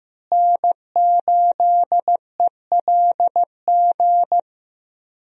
Morse code, heard as a keyed tone, about 15 wpm, 700 Hz. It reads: N8ELG
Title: USA Call Signs - 5 Characters - Repeated 3x - 15wpm